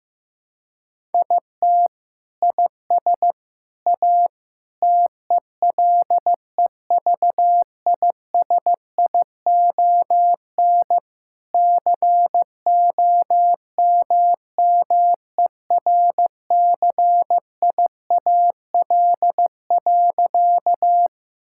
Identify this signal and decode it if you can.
Morse code, heard as a keyed tone, about 15 wpm, 700 Hz. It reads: IT IS A TELEVISION COMMERCIAL.
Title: Sentences from Top 1000 Words - Review Part 5 - 15wpm